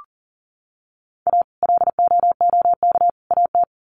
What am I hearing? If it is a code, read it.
Morse code, heard as a keyed tone, about 40 wpm, 700 Hz. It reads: A LOOK AT
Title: Sets of 3 Words 40wpm